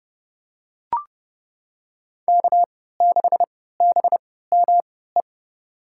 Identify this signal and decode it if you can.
Morse code, heard as a keyed tone, about 30 wpm, spaced as for 17 wpm, 700 Hz. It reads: K6BME
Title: USA Call Signs - 5 Characters - 3x Character Spacing - 30wpm